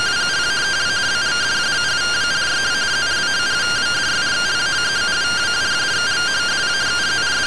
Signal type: two-tone teleprinter signal

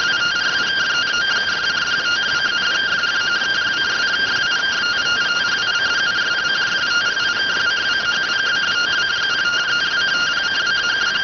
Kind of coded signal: two-tone teleprinter signal

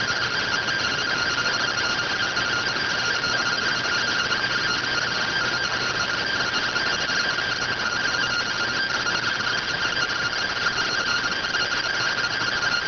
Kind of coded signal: two-tone teleprinter signal

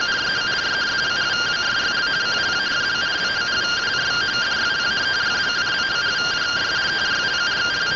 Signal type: two-tone teleprinter signal

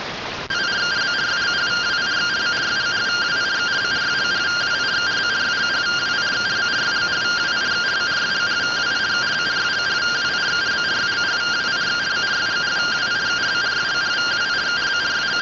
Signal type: two-tone teleprinter signal

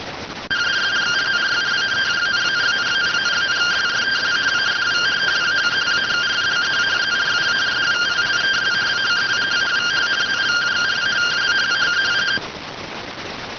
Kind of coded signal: two-tone teleprinter signal

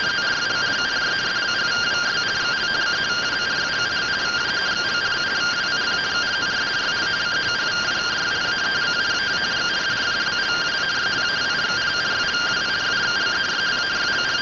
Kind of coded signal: two-tone teleprinter signal

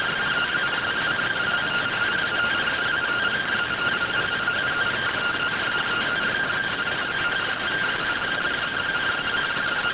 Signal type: two-tone teleprinter signal

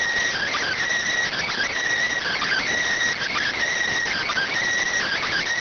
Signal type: Emergency Alert System (SAME) tones

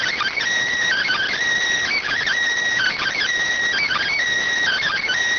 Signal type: Emergency Alert System (SAME) tones

